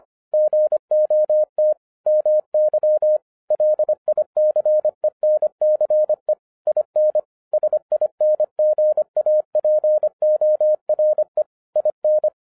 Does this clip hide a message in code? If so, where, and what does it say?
Yes: Morse code from 0.3 to 12.3 s, heard as a keyed tone, about 25 wpm, 610 Hz. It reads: GOT MY LICENCE IN SINGAPORE IN